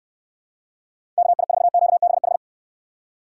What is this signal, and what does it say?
Morse code, heard as a keyed tone, about 69 wpm, 700 Hz. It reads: KI4QBH